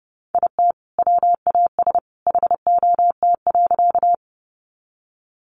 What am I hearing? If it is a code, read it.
Morse code, heard as a keyed tone, about 30 wpm, 700 Hz. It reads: IT WAS HOT.
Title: Sentences from Top 500 Words 30wpm